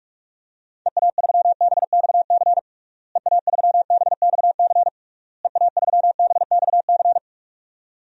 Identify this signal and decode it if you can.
Morse code, heard as a keyed tone, about 41 wpm, 700 Hz. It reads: EA3BXC EA3BXC EA3BXC
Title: DXCC-100 Call Signs - 6 Characters - Repeated 3x - 45wpm